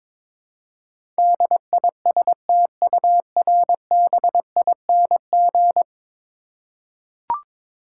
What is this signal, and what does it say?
Morse code, heard as a keyed tone, about 22 wpm, 700 Hz. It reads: DISTURBING
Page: morse-code-ninja-practice-22wpm